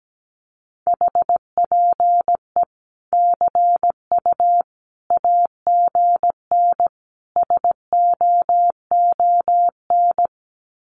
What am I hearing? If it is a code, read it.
Morse code, heard as a keyed tone, about 17 wpm, 700 Hz. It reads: HPE CU AGN SOON
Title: Common QSO Phrases 17wpm